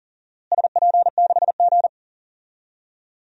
Morse code, heard as a keyed tone, about 40 wpm, 700 Hz. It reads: SP6G
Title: DXCC-100 Call Signs - 4 Characters - Repeated 2x - 40wpm